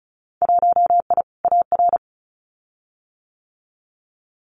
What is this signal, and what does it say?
Morse code, heard as a keyed tone, about 36 wpm, 700 Hz. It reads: 1I AR
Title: Winter Field Day Exchanges Encore 35wpm